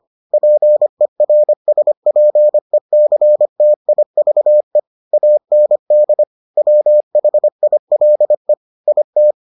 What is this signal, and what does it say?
Morse code, heard as a keyed tone, about 25 wpm, 600 Hz. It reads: PERSPECTIVE AND WHILE IT